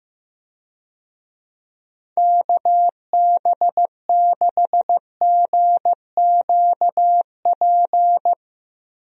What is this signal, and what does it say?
Morse code, heard as a keyed tone, about 15 wpm, 700 Hz. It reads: KB6GQP
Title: USA Call Signs - 6 Characters - 15wpm